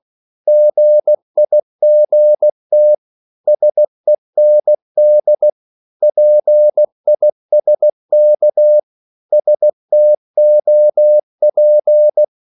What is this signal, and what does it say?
Morse code, heard as a keyed tone, about 16 wpm, 600 Hz. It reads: GIGT SEND PISK STOP
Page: Korte danske ord 16wpm